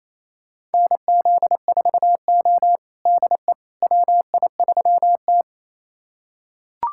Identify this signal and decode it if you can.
Morse code, heard as a keyed tone, about 28 wpm, 700 Hz. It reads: NZ4O DE WI3T
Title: USA Callsign DE Callsign - 4 Characters - 28wpm